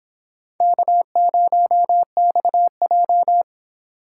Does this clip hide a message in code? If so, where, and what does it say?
Yes: Morse code from 0.6 to 3.4 s, heard as a keyed tone, about 25 wpm, 700 Hz. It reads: K0XJ